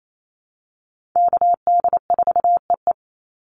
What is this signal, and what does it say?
Morse code, heard as a keyed tone, about 28 wpm, 700 Hz. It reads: KD4EE
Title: Call Signs Repeated 3x - 28wpm